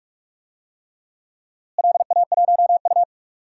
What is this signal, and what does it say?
Morse code, heard as a keyed tone, about 42 wpm, 700 Hz. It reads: LA1U
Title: DXCC-100 Call Signs - 4 Characters - Repeated 3x - 45wpm